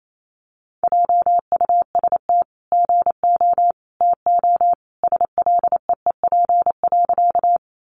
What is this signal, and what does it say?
Morse code, heard as a keyed tone, about 28 wpm, 700 Hz. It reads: JUST GO TO SLEEP.
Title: Sentences from Top 500 Words 28wpm